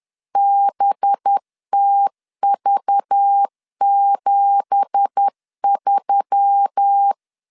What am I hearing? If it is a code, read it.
Morse code, heard as a keyed tone, about 10 wpm, 800 Hz. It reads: BTV73
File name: CW / DET / 1 9 1500C S B 10wpm 4439